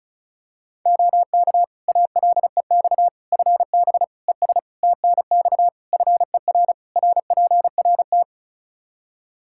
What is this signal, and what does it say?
Morse code, heard as a keyed tone, about 34 wpm, 700 Hz. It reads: OK ALEX FB ES TNX FER RPRT
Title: Practice QSO - Exchange 2 - Info - 35wpm